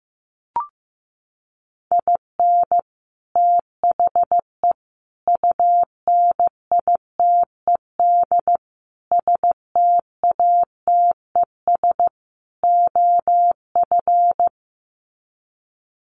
Morse code, heard as a keyed tone, about 15 wpm, 700 Hz. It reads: IN THE UNITED STATES OF
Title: Sets of 5 Words Encore2 15wpm